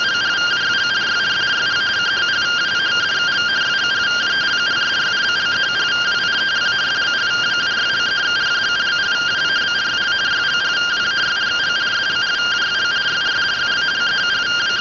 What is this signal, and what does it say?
Two-tone teleprinter signal, about 170 Hz shift.
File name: findings / SAGITAL nb203 fr61 3mm.gen-15 rtty rx